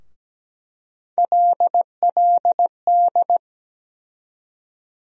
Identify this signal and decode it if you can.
Morse code, heard as a keyed tone, about 17 wpm, 700 Hz. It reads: LLD
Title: USA Call Sign Postfixes - 3 Letters 17wpm